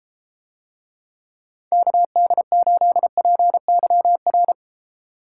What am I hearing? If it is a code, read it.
Morse code, heard as a keyed tone, about 32 wpm, 700 Hz. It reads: KD8PYR